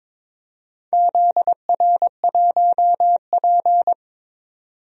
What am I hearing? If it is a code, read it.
Morse code, heard as a keyed tone, about 22 wpm, 700 Hz. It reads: ZR1P
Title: DXCC-100 Call Signs - 4 Characters - 22wpm